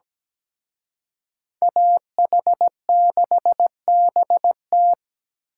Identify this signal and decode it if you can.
Morse code, heard as a keyed tone, about 20 wpm, 700 Hz. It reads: AH6BT
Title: Speed Racing - USA 2x2 Call Signs 17wpm